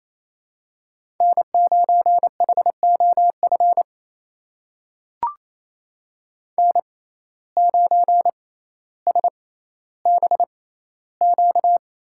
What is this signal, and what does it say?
Morse code, heard as a keyed tone, about 29 wpm, spaced as for 14 wpm, 700 Hz. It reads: N9HOF N9SBQ
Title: USA Call Signs - 5 Characters - 6x Character Spacing - 28wpm